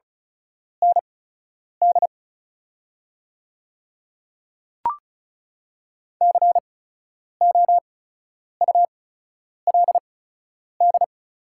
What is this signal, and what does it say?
Morse code, heard as a keyed tone, about 34 wpm, spaced as for 10 wpm, 690 Hz. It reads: ND COULD
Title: 8x Character Spacing - Top 200 Words - 35wpm